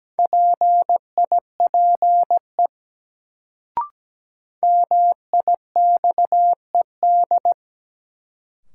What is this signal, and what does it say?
Morse code, heard as a keyed tone, about 17 wpm, 700 Hz. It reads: PIPE MIXED
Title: Sentences from Top 1500 Words - New Word Drill Alt - 17wpm